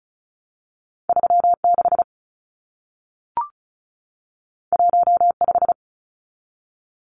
Morse code, heard as a keyed tone, about 36 wpm, 700 Hz. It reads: 36 15
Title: Numbers - 2-Digit 35wpm